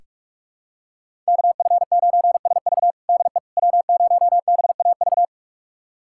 Morse code, heard as a keyed tone, about 46 wpm, 700 Hz. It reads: KF9SV DE W0BAV
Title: USA Callsign DE Callsign - 5 Characters - 45wpm